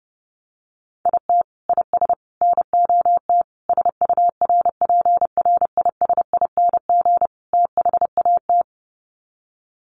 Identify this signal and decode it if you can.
Morse code, heard as a keyed tone, about 30 wpm, 700 Hz. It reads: IT IS NOT SURPRISING THAT
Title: Sets of 5 Words 30wpm